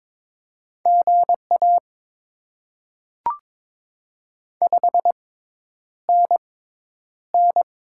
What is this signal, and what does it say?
Morse code, heard as a keyed tone, about 22 wpm, spaced as for 10 wpm, 700 Hz. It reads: GA 5NN
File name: Top 20 QSO Elements - 6x Character Spacing - 22wpm